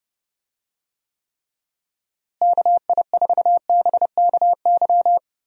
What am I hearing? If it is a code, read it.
Morse code, heard as a keyed tone, about 31 wpm, 700 Hz. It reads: KI4BKY
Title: USA Call Signs - 6 Characters - Repeated 2x - 30wpm